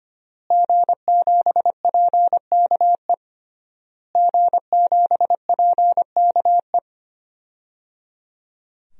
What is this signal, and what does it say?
Morse code, heard as a keyed tone, about 25 wpm, 700 Hz. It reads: G7PKE G7PKE
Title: DXCC-100 Call Signs - 5 Characters - Repeated 3x - 25wpm